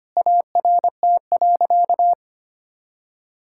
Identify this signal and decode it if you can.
Morse code, heard as a keyed tone, about 25 wpm, 700 Hz. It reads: ART.
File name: practice / 25wpm / Sentences with 2-Word Phrases - 25wpm